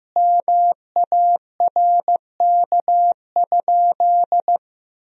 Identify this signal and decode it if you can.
Morse code, heard as a keyed tone, about 15 wpm, 700 Hz. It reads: MARK?
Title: Sentences from Top 1000 Words - Review Part 1 - 15wpm